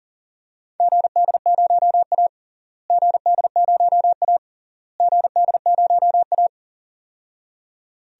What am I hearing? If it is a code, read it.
Morse code, heard as a keyed tone, about 40 wpm, 700 Hz. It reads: GD0A GD0A GD0A
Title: DXCC-100 Call Signs - 4 Characters - Repeated 3x - 40wpm